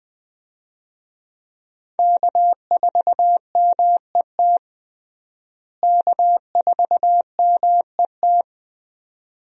Morse code, heard as a keyed tone, about 20 wpm, 700 Hz. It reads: K4MET K4MET
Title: USA Call Signs - 5 Characters - Repeated 2x - 20wpm